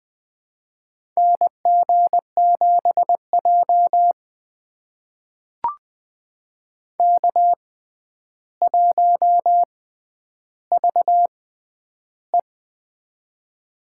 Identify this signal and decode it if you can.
Morse code, heard as a keyed tone, about 20 wpm, spaced as for 10 wpm, 700 Hz. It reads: NG7J K1VE
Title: USA Call Signs - 4 Characters - 6x Character Spacing - 20wpm